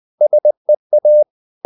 Morse code, heard as a keyed tone, about 20 wpm, 600 Hz. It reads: SEA
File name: the old man and the sea / the old man and the sea part 1 20wpm